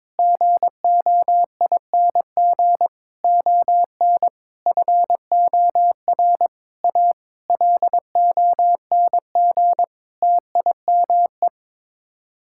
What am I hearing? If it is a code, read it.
Morse code, heard as a keyed tone, about 22 wpm, 700 Hz. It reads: GOING ON FOR A LONG TIME
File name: practice / 22wpm / Sets of 8 Words - 22wpm